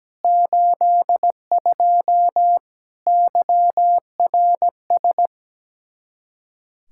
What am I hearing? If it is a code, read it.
Morse code, heard as a keyed tone, about 17 wpm, 700 Hz. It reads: 82 YRS
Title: K1USN SST QSO Practice 17wpm